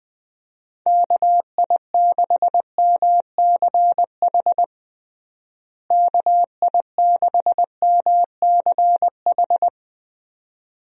Morse code, heard as a keyed tone, about 21 wpm, 700 Hz. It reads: KI6MCH KI6MCH
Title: USA Call Signs - 6 Characters - Repeated 2x - 20wpm